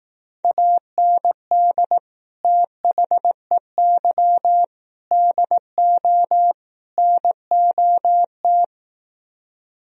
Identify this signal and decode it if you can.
Morse code, heard as a keyed tone, about 18 wpm, 700 Hz. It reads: AND THEY DO NOT
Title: Speed-Racing - Sets of 4 Words 15wpm